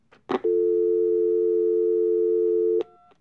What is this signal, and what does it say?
Dial tone (North America), 350 + 440 Hz.
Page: telephone-electronic-dial-tone-punch-in-numbers-internal-office